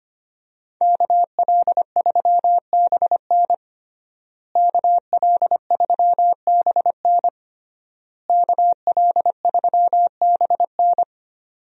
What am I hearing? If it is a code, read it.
Morse code, heard as a keyed tone, about 25 wpm, 700 Hz. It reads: KL3BN KL3BN KL3BN
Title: USA Call Signs - 5 Characters - Repeated 3x - 25wpm